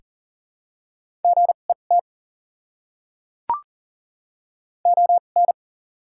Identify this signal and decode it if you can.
Morse code, heard as a keyed tone, about 40 wpm, spaced as for 29 wpm, 700 Hz. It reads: GET ON
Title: 2x Character Spacing - Top 200 Words - 40wpm